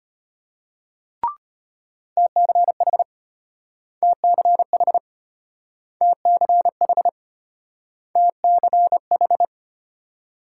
Morse code, heard as a keyed tone, about 33 wpm, 700 Hz. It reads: TCH TCH TCH TCH